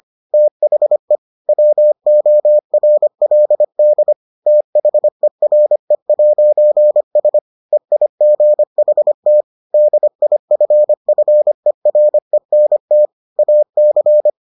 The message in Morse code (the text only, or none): THE WORLD THERE'S EIGHT DIFFERENT AC